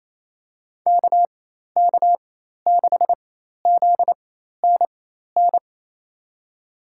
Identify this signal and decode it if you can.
Morse code, heard as a keyed tone, about 28 wpm, spaced as for 13 wpm, 700 Hz. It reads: KK6ZNN
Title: USA Call Signs - 6 Characters - 4x Character Spacing - 28wpm